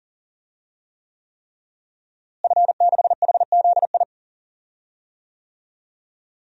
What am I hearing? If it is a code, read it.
Morse code, heard as a keyed tone, about 40 wpm, 700 Hz. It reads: F6HZI
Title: DXCC-100 Call Signs - 5 Characters - Repeated 3x - 40wpm